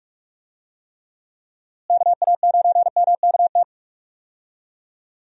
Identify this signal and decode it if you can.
Morse code, heard as a keyed tone, about 46 wpm, 700 Hz. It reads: KA9MKT
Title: USA Call Signs - 6 Characters - Repeated 2x - 45wpm